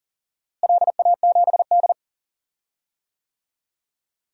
Morse code, heard as a keyed tone, about 40 wpm, 690 Hz. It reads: LA7D
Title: DXCC-100 Call Signs - 4 Characters - Repeated 2x - 40wpm